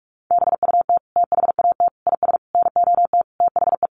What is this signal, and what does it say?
Morse code, heard as a keyed tone, about 44 wpm, 700 Hz. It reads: BUT THAT IS NOT THE
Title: Sentences from Top 600 Words 45wpm